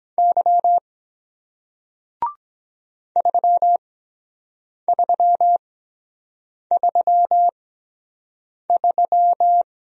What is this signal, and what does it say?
Morse code, heard as a keyed tone, about 23 wpm, spaced as for 7 wpm, 700 Hz. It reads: Y 3333